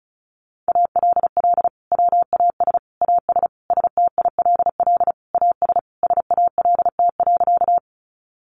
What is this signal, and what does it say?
Morse code, heard as a keyed tone, about 34 wpm, 700 Hz. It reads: ALL WAS AS STILL AS SALT.
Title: Sentences from Top 800 Words 35wpm